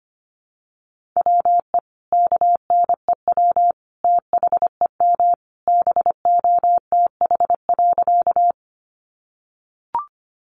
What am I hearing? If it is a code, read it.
Morse code, heard as a keyed tone, about 25 wpm, 700 Hz. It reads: WE KNEW THEM BOTH.
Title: Sentences from Top 400 Words 25wpm